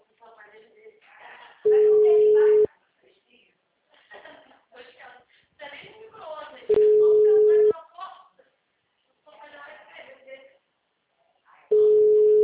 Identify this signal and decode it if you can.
Ringback tone (Europe, 425 Hz), with speech.
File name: out- -2037- - - .24782